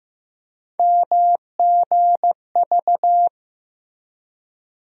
Morse code, heard as a keyed tone, about 15 wpm, 700 Hz. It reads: MGV
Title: USA Call Sign Postfixes - 3 Letters 15wpm